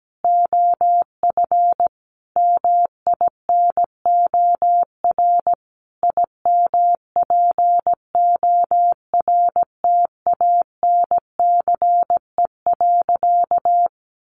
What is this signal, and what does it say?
Morse code, heard as a keyed tone, about 17 wpm, 700 Hz. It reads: OF MINOR IMPORTANCE.
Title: Sentences from Top 1500 Words 17wpm